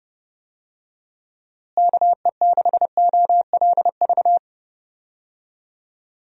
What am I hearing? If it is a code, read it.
Morse code, heard as a keyed tone, about 32 wpm, 700 Hz. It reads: KE6OLV
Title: USA Call Signs - 6 Characters - Repeated 3x - 30wpm